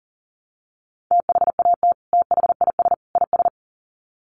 Morse code, heard as a keyed tone, about 40 wpm, 700 Hz. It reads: THAT THIS IS
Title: Sets of 3 Words 40wpm